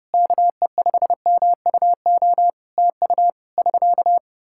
Morse code, heard as a keyed tone, about 30 wpm, 700 Hz. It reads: KE5MUO TU <SK>
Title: K1USN SST QSO Practice 30wpm